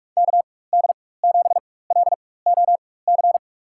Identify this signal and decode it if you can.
Morse code, heard as a keyed tone, about 46 wpm, spaced as for 21 wpm, 700 Hz. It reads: KD7LOC
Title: USA Call Signs - 6 Characters - 4x Character Spacing - 45wpm